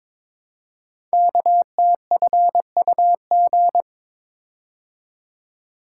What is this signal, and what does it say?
Morse code, heard as a keyed tone, about 23 wpm, 700 Hz. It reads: KTFUG
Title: Code Groups - 5 Letters - 22wpm